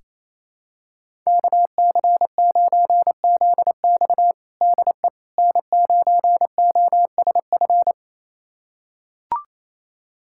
Morse code, heard as a keyed tone, about 28 wpm, 700 Hz. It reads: KC9ZX DE N9OSF
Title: USA Callsign DE Callsign - 5 Characters - 28wpm